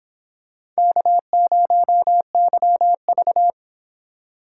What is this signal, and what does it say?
Morse code, heard as a keyed tone, about 24 wpm, 700 Hz. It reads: K0YV